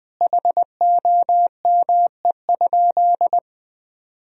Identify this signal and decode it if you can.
Morse code, heard as a keyed tone, about 20 wpm, 700 Hz. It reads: HOME?
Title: Sentences from Top 1000 Words - Review Part 2 - 20wpm